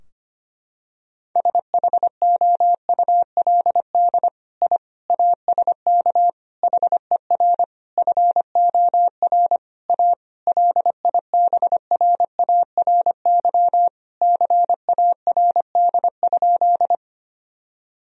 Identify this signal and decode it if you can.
Morse code, heard as a keyed tone, about 25 wpm, 700 Hz. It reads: SHOULD I ASK HER FOR A LIBRARY CARD?
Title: Sentences from Top 1000 Words - Review Part 4 - 25wpm